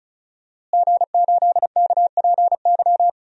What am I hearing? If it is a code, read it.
Morse code, heard as a keyed tone, about 36 wpm, 700 Hz. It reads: G8KPY
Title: DXCC-100 Call Signs - 5 Characters - Repeated 2x - 35wpm